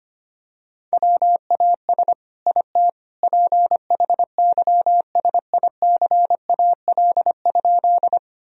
Morse code, heard as a keyed tone, about 25 wpm, 700 Hz. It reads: WAS IT PHYSICAL?
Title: Sentences from Top 1000 Words - Review Part 4 - 25wpm